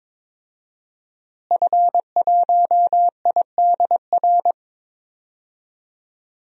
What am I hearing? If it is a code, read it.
Morse code, heard as a keyed tone, about 22 wpm, 700 Hz. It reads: F1IDR
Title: DXCC-100 Call Signs - 5 Characters - Repeated 3x - 22wpm